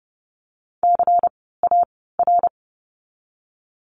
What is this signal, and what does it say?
Morse code, heard as a keyed tone, about 30 wpm, spaced as for 17 wpm, 700 Hz. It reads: CAR
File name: Lesson 16f - Letter c - Words 30wpm